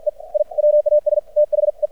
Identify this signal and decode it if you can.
Morse code, heard as a keyed tone, about 53 wpm, 600 Hz. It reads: E I WAS THE
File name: archived / CRSnet